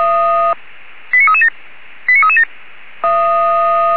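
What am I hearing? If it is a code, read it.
Five-tone selective call (ZVEI), heard as quick runs of five tones, tones of about 70 ms. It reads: 89337 89337
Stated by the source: ZVEI five-tone selective call received on SDR.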